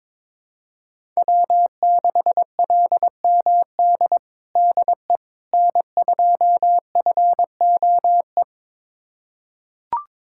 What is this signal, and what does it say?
Morse code, heard as a keyed tone, about 21 wpm, 700 Hz. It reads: W6LMD DE N2FOE
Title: USA Callsign DE Callsign - 5 Characters - 22wpm